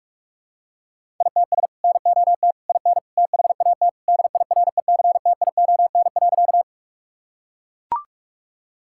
Morse code, heard as a keyed tone, about 41 wpm, 700 Hz. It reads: ITS NOT IN THAT DIRECTION.
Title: Sentences from Top 1000 Words - Review Part 5 - 45wpm